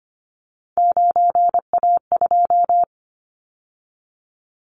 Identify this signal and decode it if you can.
Morse code, heard as a keyed tone, about 25 wpm, 700 Hz. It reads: 9A2
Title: Callsign Prefixes for Top 100 DXCC - 25wpm